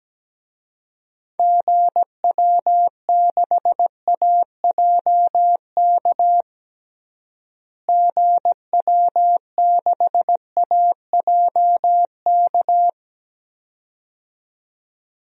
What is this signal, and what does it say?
Morse code, heard as a keyed tone, about 17 wpm, 700 Hz. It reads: GW6AJK GW6AJK
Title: DXCC-100 Call Signs - 6 Characters - Repeated 2x - 17wpm